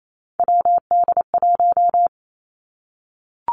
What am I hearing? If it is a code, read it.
Morse code, heard as a keyed tone, about 28 wpm, 700 Hz. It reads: WD1
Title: Three Challenging Letter-Number Combinations 28wpm